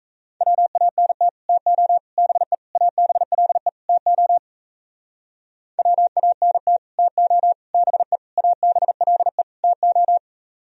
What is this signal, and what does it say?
Morse code, heard as a keyed tone, about 38 wpm, 700 Hz. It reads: WANT TO BE ABLE TO WANT TO BE ABLE TO